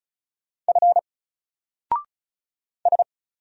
Morse code, heard as a keyed tone, about 35 wpm, 700 Hz. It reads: F S
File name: Single-Letter-Number-35wpm